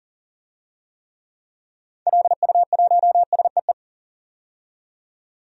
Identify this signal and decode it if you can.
Morse code, heard as a keyed tone, about 40 wpm, 700 Hz. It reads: LU1SEE
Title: DXCC-100 Call Signs - 6 Characters - 40wpm